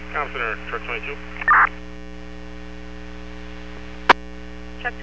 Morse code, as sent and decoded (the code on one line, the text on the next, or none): none